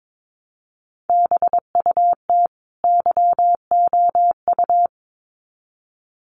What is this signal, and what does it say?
Morse code, heard as a keyed tone, about 22 wpm, 700 Hz. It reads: BUT YOU
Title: Sets of 2 Words 22wpm